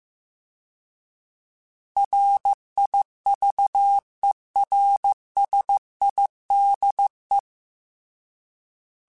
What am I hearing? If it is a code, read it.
Morse code, heard as a keyed tone, about 15 wpm, 800 Hz. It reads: RIVERSIDE